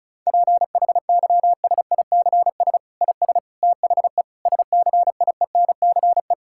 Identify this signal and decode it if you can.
Morse code, heard as a keyed tone, about 36 wpm, 700 Hz. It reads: PHYSICS IS THE SCIENCE
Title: Sentences from Top 1000 Words - Review Part 5 - 35wpm